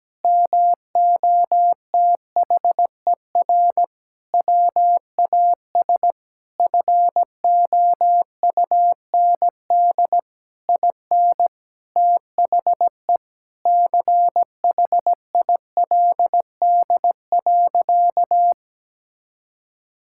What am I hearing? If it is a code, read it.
Morse code, heard as a keyed tone, about 17 wpm, 700 Hz. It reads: MOTHER WAS FOUND IN THE CHILD.
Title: Sentences from Top 1000 Words - Review Part 5 - 17wpm